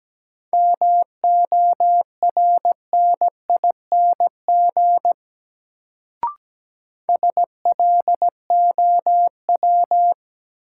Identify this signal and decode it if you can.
Morse code, heard as a keyed tone, about 17 wpm, 700 Hz. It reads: MORNING SLOW
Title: Sentences from Top 400 Words - New Word Drill - 17wpm